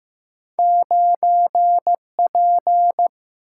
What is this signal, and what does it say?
Morse code, heard as a keyed tone, about 15 wpm, 700 Hz. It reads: 9 P
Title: Single-Letter Single-Number 15wpm